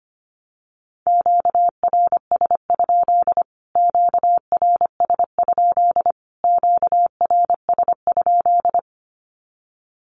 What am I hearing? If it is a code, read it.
Morse code, heard as a keyed tone, about 25 wpm, 700 Hz. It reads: QRS? QRS? QRS?
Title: Q-Codes Repeated 3x 25wpm